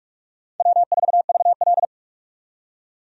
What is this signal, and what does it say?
Morse code, heard as a keyed tone, about 41 wpm, 700 Hz. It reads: W4VL
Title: USA Call Signs - 4 Characters - 45wpm